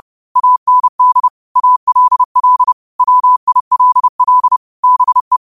Morse code, heard as a keyed tone, about 30 wpm, 1000 Hz. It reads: AND ALL WILL BE
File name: qotdfiles / 30wpm